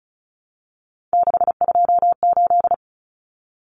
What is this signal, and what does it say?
Morse code, heard as a keyed tone, about 34 wpm, 700 Hz. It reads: 628
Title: Numbers - 3-Digit 35wpm